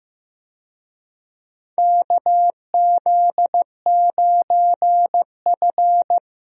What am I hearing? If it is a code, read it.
Morse code, heard as a keyed tone, about 15 wpm, 700 Hz. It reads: KZ9F
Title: USA Call Signs - 4 Characters - Repeated 2x - 15wpm